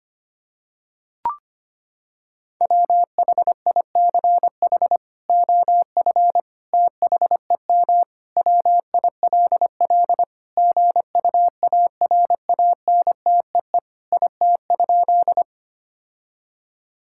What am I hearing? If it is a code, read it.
Morse code, heard as a keyed tone, about 25 wpm, 700 Hz. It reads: WHICH OF THEM WILL GUARANTEE IT?
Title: Sentences from Top 1600 Words 25wpm